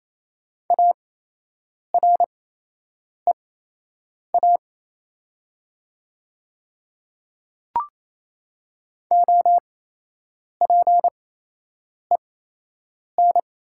Morse code, heard as a keyed tone, about 28 wpm, spaced as for 8 wpm, 700 Hz. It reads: AREA OPEN
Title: 8x Character Spacing - Top 500 Words - 28wpm